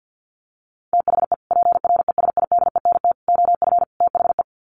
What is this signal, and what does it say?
Morse code, heard as a keyed tone, about 50 wpm, 700 Hz. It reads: THE PRESIDENT OF THE
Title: Sets of 4 Words 50wpm